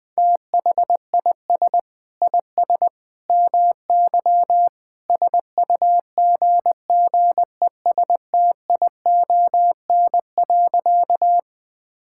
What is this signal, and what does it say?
Morse code, heard as a keyed tone, about 20 wpm, 700 Hz. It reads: THIS IS MY SUGGESTION.
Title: Sentences from Top 1600 Words 20wpm